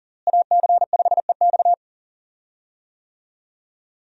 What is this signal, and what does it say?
Morse code, heard as a keyed tone, about 40 wpm, 700 Hz. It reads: AC5EX
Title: USA Call Signs - 5 Characters - Repeated 2x - 40wpm